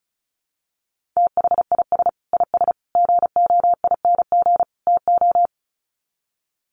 Morse code, heard as a keyed tone, about 34 wpm, 700 Hz. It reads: THIS IS GOING TO
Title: Sets of 4 Words 35wpm